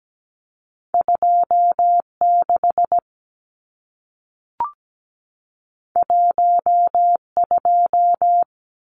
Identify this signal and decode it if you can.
Morse code, heard as a keyed tone, about 17 wpm, 700 Hz. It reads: 26 12
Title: Numbers - 2-Digit 17wpm